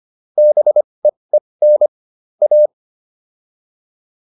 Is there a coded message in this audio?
Yes: Morse code from 0.4 to 2.7 s, heard as a keyed tone, about 25 wpm, spaced as for 20 wpm, 600 Hz. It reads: BEEN A